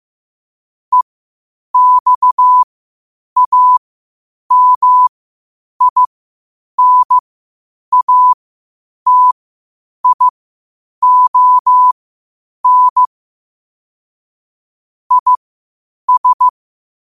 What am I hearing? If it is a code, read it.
Morse code, heard as a keyed tone, about 15 wpm, spaced as for 8 wpm, 1000 Hz. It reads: EXAMINATION IS
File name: qotdfiles / 8wpm